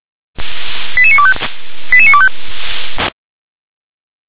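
Five-tone selective call (ZVEI), heard as quick runs of five tones, tones of about 70 ms. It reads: 88925 88925
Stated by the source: Five-tone ZVEI selective call received by SDR.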